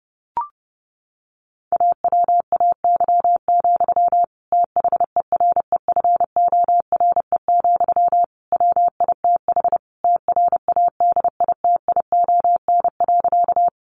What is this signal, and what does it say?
Morse code, heard as a keyed tone, about 30 wpm, 700 Hz. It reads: AWAY, THEREFORE, WITH TRADITION.
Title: Sentences from Top 1500 Words 30wpm